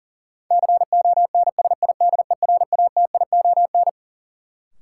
Morse code, heard as a keyed tone, about 40 wpm, 700 Hz. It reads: CONSIDERATION
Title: Sentences from Top 1500 Words - New Word Drill - 40wpm